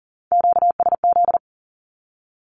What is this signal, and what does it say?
Morse code, heard as a keyed tone, about 40 wpm, 700 Hz. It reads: QSZ
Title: Q-Codes Repeated 3x 40wpm